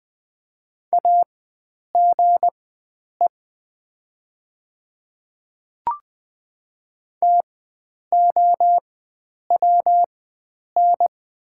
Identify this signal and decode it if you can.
Morse code, heard as a keyed tone, about 20 wpm, spaced as for 9 wpm, 700 Hz. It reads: AGE TOWN
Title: 4x Character Spacing - Top 500 Words - 20wpm